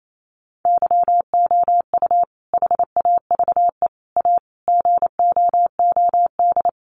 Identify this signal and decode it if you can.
Morse code, heard as a keyed tone, about 28 wpm, 700 Hz. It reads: YOU HAVE A GOOD
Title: Sentences from Top 1300 Words 28wpm